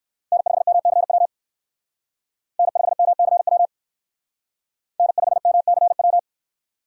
Morse code, heard as a keyed tone, about 57 wpm, 700 Hz. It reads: N5MCW N5MCW N5MCW